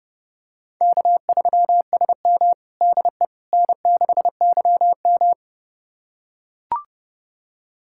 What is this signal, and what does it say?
Morse code, heard as a keyed tone, about 30 wpm, 700 Hz. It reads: K3SM DE N6YM
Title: USA Callsign DE Callsign - 4 Characters - 30wpm